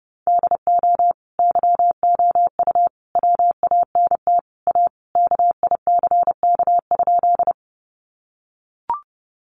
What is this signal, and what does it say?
Morse code, heard as a keyed tone, about 28 wpm, 700 Hz. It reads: DO YOU WANT A KICK?
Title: Sentences from Top 1300 Words 30wpm